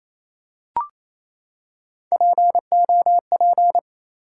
Morse code, heard as a keyed tone, about 28 wpm, 700 Hz. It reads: POP
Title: Three Letter Words - 28wpm